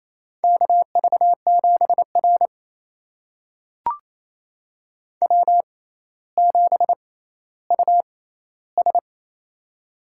Morse code, heard as a keyed tone, about 28 wpm, spaced as for 14 wpm, 700 Hz. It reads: KV7R W7US
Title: USA Call Signs - 4 Characters - 6x Character Spacing - 28wpm